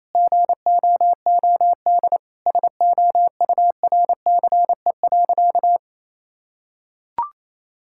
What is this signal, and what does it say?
Morse code, heard as a keyed tone, about 28 wpm, 700 Hz. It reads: GOOD SOURCE.
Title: Sentences from Top 1000 Words - Review Part 5 - 28wpm